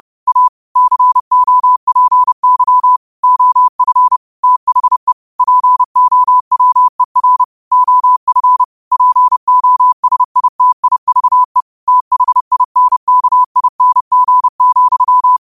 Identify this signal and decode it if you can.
Morse code, heard as a keyed tone, about 30 wpm, 1000 Hz. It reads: A COPY OF THE POWER OF POSITIVE THINKING,
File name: qotdfiles / 30wpm